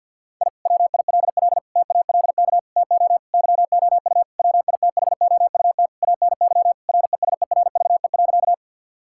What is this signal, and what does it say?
Morse code, heard as a keyed tone, about 50 wpm, 700 Hz. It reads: I WILL TALK TO YOU WITHOUT ANY RESERVE.
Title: Sentences from Top 1000 Words - Review Part 6 - 50wpm